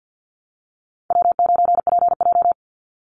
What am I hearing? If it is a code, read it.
Morse code, heard as a keyed tone, about 50 wpm, 700 Hz. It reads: W9PJ
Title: Easy Call Signs - 50wpm